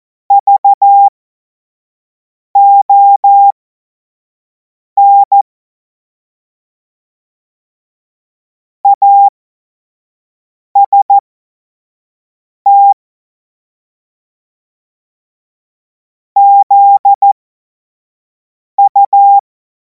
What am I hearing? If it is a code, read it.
Morse code, heard as a keyed tone, about 14 wpm, spaced as for 5 wpm, 800 Hz. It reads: VON AST ZU
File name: Morsekurs / Witze / Witz 14. Juli 2026 25 F70